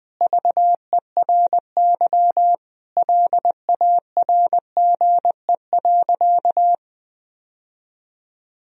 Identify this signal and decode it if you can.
Morse code, heard as a keyed tone, about 20 wpm, 700 Hz. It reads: VERY LARGE.
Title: Sentences from Top 1000 Words - Review Part 1 - 20wpm